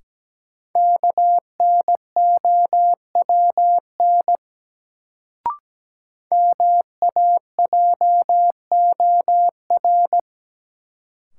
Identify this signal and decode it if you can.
Morse code, heard as a keyed tone, about 17 wpm, 700 Hz. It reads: KNOWN MAJOR
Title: Sentences from Top 600 Words - New Word Drill - 17wpm